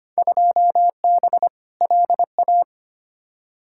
Morse code, heard as a keyed tone, about 25 wpm, 700 Hz. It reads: 2B LA
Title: Field Day Exchanges Encore - 25wpm